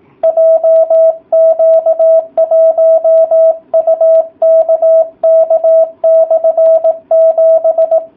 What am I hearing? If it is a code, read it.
Morse code, heard as a keyed tone, about 18 wpm, 660 Hz. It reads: JQ1UKK/7